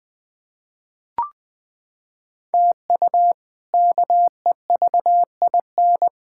Morse code, heard as a keyed tone, about 20 wpm, 700 Hz. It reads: TU KEVIN
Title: K1USN SST QSO Practice - 20wpm